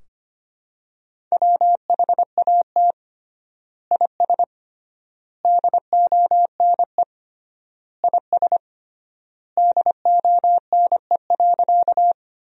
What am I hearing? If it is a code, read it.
Morse code, heard as a keyed tone, about 25 wpm, 700 Hz. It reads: WHAT IS DONE IS DONE.
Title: 3x Word-Spacing - Sentences from Top 500 Words - 25wpm